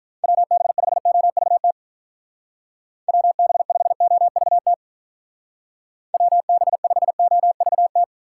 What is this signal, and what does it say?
Morse code, heard as a keyed tone, about 43 wpm, 700 Hz. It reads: WB5OVT WB5OVT WB5OVT
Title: Speed Racing - USA Call Signs 35wpm